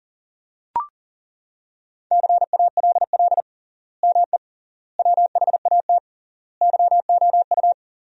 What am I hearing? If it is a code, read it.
Morse code, heard as a keyed tone, about 40 wpm, 700 Hz. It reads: CALL ME WHAT YOU
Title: 3x Word-Spacing - Sentences from Top 200 Words - 40wpm